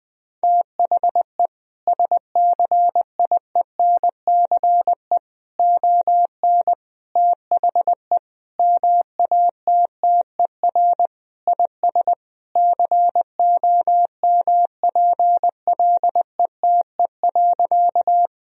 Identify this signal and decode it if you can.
Morse code, heard as a keyed tone, about 21 wpm, 700 Hz. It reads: THE SCIENCE ON THE MATTER IS COMPLETE.
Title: Sentences from Top 1000 Words - Review Part 2 - 20wpm